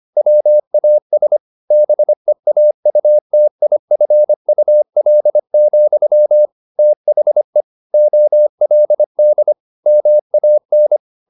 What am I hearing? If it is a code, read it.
Morse code, heard as a keyed tone, about 25 wpm, 600 Hz. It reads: WAS BEAUTIFUL, THE OLD MAN
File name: the old man and the sea part 1 25wpm